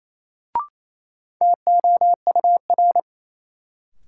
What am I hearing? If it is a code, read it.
Morse code, heard as a keyed tone, about 28 wpm, 700 Hz. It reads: TOUR
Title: Sentences from Top 1400 Words - New Word Drill Alt - 28wpm